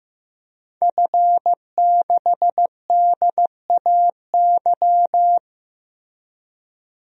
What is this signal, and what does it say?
Morse code, heard as a keyed tone, about 15 wpm, 700 Hz. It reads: F6DAY
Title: DXCC-100 Call Signs - 5 Characters - Repeated 3x - 15wpm